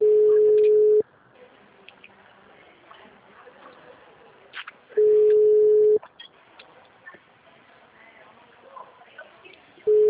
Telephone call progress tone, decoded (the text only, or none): ringback tone